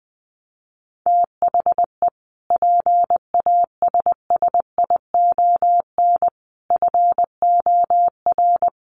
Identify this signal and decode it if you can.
Morse code, heard as a keyed tone, about 20 wpm, 700 Hz. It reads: THE PASSION FOR
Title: Sentences from Top 1300 Words 20wpm